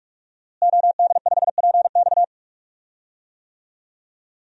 Morse code, heard as a keyed tone, about 47 wpm, 700 Hz. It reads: OD5PX
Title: DXCC-100 Call Signs - 5 Characters - Repeated 3x - 45wpm